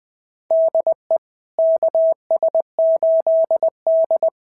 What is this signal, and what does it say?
Morse code, heard as a keyed tone, about 20 wpm, 660 Hz. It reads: DE KS8D
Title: SOTA QSO Practice 20wpm